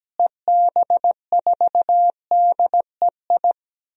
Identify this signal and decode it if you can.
Morse code, heard as a keyed tone, about 17 wpm, 700 Hz. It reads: EB4DEI
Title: DXCC-100 Call Signs - 6 Characters - Repeated 3x - 17wpm